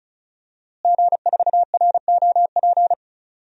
Morse code, heard as a keyed tone, about 36 wpm, 700 Hz. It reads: G4ROP
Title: DXCC-100 Call Signs - 5 Characters - Repeated 2x - 35wpm